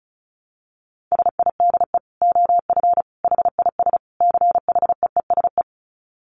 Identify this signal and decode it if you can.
Morse code, heard as a keyed tone, about 34 wpm, 700 Hz. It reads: SIDE OF HIS CHEESE
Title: Dit Challenge 35wpm